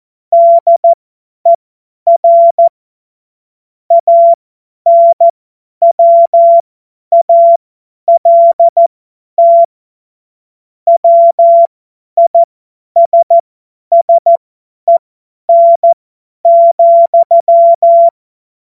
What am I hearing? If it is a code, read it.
Morse code, heard as a keyed tone, about 14 wpm, spaced as for 10 wpm, 680 Hz. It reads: DER ANWALT WISSEN,
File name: Morsekurs / Witze / Witz 23. August 2022 50 F70